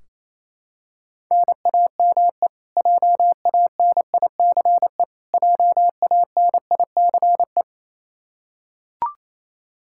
Morse code, heard as a keyed tone, about 27 wpm, 700 Hz. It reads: NAME JANICE JANICE
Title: Practice QSO - Exchange 1 - Beginning - 28wpm